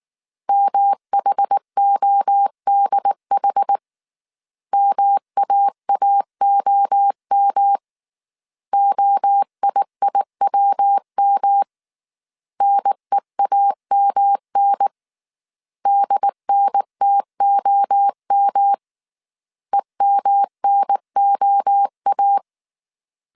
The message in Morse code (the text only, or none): MHODH MAAOM OIIWM NEAMN DNTOM EMNOA